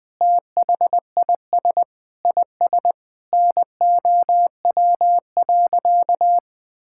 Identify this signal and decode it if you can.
Morse code, heard as a keyed tone, about 21 wpm, 700 Hz. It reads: THIS IS NOW.
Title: Sentences with 8-Word Phrases - Spoken Beginning - 20wpm